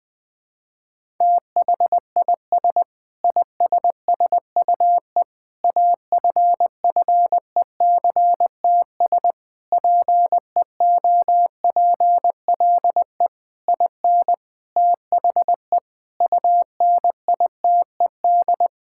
Morse code, heard as a keyed tone, about 20 wpm, 700 Hz. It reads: THIS ISSUE AFFECTS PEOPLE IN THE UNITED
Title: Sentences with 8-Word Phrases - 20wpm